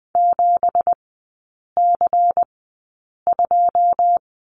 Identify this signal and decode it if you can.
Morse code, heard as a keyed tone, about 20 wpm, spaced as for 8 wpm, 700 Hz. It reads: 7C2
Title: 198 Lesson 32e - Letter k - Three Characters 20wpm